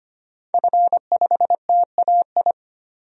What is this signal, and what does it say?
Morse code, heard as a keyed tone, about 25 wpm, 700 Hz. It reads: F5TAI
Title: DXCC-100 Call Signs - 5 Characters - 25wpm